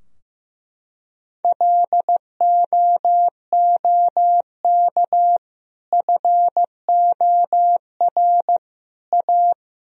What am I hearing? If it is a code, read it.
Morse code, heard as a keyed tone, about 15 wpm, 700 Hz. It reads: LOOK FOR A
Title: Sentences from Top 1000 Words - Review Part 2 - 15wpm